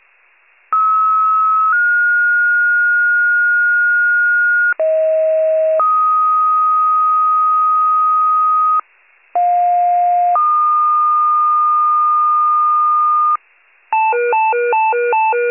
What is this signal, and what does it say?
Two-tone page, heard as a one-second tone, then a three-second tone, about 1290 then 1430 Hz, about 640 then 1250 Hz, about 710 then 1220 Hz. It is heard from 0.7 to 13.4 s.